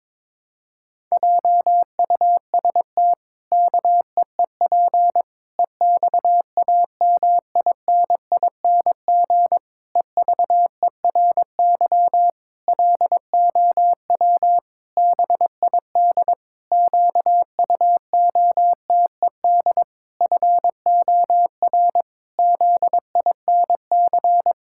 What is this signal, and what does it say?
Morse code, heard as a keyed tone, about 22 wpm, 700 Hz. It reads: JUST KEEP EXAMINING EVERY LOW BID QUOTED FOR ZINC
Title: Pangram Challenge v2 - 22wpm